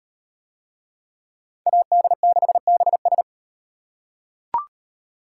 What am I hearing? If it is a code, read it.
Morse code, heard as a keyed tone, about 36 wpm, 700 Hz. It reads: AD6BS